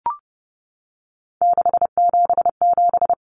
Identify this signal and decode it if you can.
Morse code, heard as a keyed tone, about 30 wpm, 700 Hz. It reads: 677
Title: Numbers - 3-Digit 30wpm